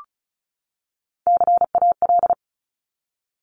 Morse code, heard as a keyed tone, about 34 wpm, 700 Hz. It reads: CAL
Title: Top Three Letter Combinations in Words 35wpm